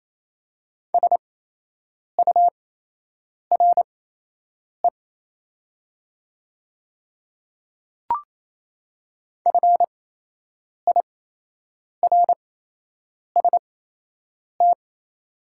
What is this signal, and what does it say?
Morse code, heard as a keyed tone, about 28 wpm, spaced as for 8 wpm, 700 Hz. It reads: SURE FIRST
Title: 8x Character Spacing - Top 500 Words - 28wpm